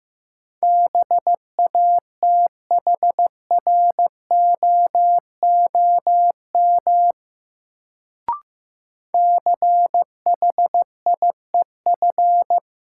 Morse code, heard as a keyed tone, about 15 wpm, 700 Hz. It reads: BATHROOM CHIEF
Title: Sets of 5 Words Encore - New Word Drill - 15wpm